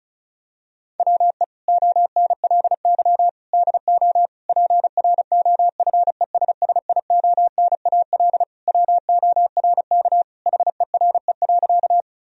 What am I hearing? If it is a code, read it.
Morse code, heard as a keyed tone, about 36 wpm, 700 Hz. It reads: WE ONLY DO PROFESSIONAL WORK HERE.
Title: Sentences from Top 1000 Words - Review Part 4 - 35wpm